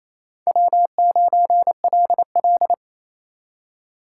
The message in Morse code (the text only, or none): W9LL